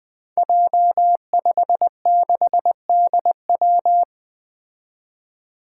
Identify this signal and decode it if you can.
Morse code, heard as a keyed tone, about 20 wpm, 700 Hz. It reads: J56DW
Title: Five Challenging Letter-Number Combinations 20wpm